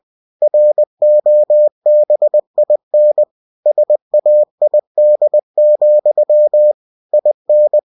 Morse code, heard as a keyed tone, about 20 wpm, 600 Hz. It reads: ROBIN SAID, IN